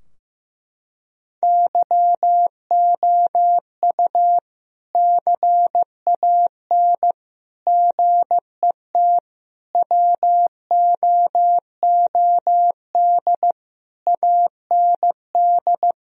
Morse code, heard as a keyed tone, about 15 wpm, 700 Hz. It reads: YOU CAN GET WOOD AND
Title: Sentences from Top 1000 Words - Review Part 2 - 15wpm